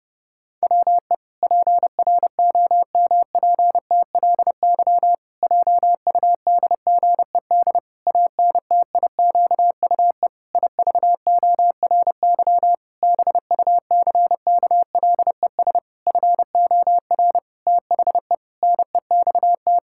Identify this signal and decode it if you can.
Morse code, heard as a keyed tone, about 30 wpm, 700 Hz. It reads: WE PROMPTLY JUDGED ANTIQUE IVORY BUCKLES FOR THE NEXT
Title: Pangram Challenge v2 - 30wpm